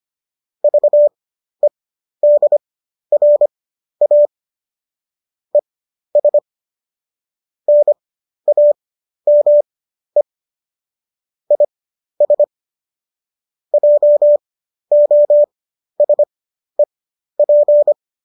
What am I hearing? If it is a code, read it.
Morse code, heard as a keyed tone, about 25 wpm, spaced as for 12 wpm, 590 Hz. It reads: VEDRA ES NAME IS JOSEP
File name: qso009 12